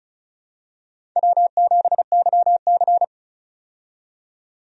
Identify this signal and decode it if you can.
Morse code, heard as a keyed tone, about 34 wpm, 700 Hz. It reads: W7YC
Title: USA Call Signs - 4 Characters - 35wpm